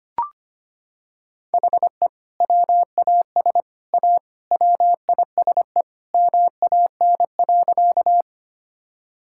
Morse code, heard as a keyed tone, about 25 wpm, 700 Hz. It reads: HE WAS A WISE MAN.
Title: Sentences from Top 1600 Words 25wpm